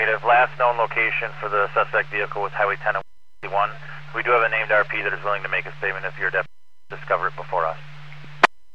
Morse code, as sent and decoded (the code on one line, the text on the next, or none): none